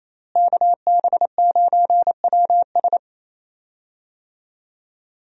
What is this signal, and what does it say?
Morse code, heard as a keyed tone, about 28 wpm, 700 Hz. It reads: KB9WS
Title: USA Call Signs - 5 Characters - Repeated 2x - 28wpm